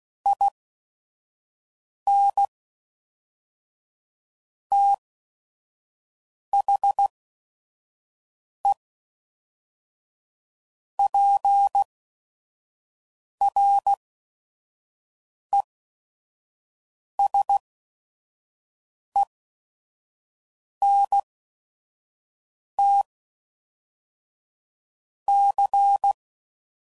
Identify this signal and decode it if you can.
Morse code, heard as a keyed tone, about 16 wpm, 800 Hz. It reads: I N T H E P R E S E N T C